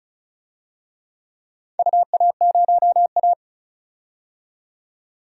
Morse code, heard as a keyed tone, about 36 wpm, 700 Hz. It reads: UA0A
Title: DXCC-100 Call Signs - 4 Characters - 35wpm